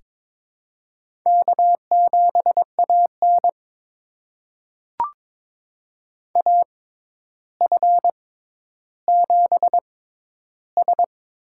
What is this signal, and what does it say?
Morse code, heard as a keyed tone, about 22 wpm, spaced as for 11 wpm, 700 Hz. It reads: K7AN AF7S
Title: USA Call Signs - 4 Characters - 6x Character Spacing - 22wpm